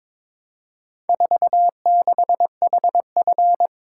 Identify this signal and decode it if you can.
Morse code, heard as a keyed tone, about 22 wpm, 700 Hz. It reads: 46HF
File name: practice / 22wpm / Four Letter-Number Combinations - 22wpm